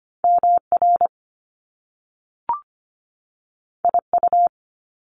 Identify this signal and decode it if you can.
Morse code, heard as a keyed tone, about 25 wpm, 700 Hz. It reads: MR IU
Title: Top Two Letter Combinations in Words 25wpm